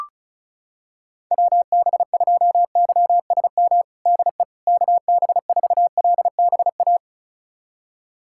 Morse code, heard as a keyed tone, about 36 wpm, 700 Hz. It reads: WB2YSM DE KB4LBA
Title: USA Callsign DE Callsign 35wpm